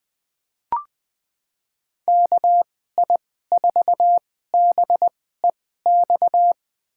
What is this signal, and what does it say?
Morse code, heard as a keyed tone, about 20 wpm, spaced as for 15 wpm, 700 Hz. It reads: KI4BEX
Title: USA Call Signs - 2x Character Spacing - 20wpm